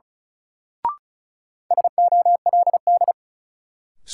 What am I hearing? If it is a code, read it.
Morse code, heard as a keyed tone, about 36 wpm, 700 Hz. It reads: SOLD
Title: Sentences from Top 1400 Words - New Word Drill - 35wpm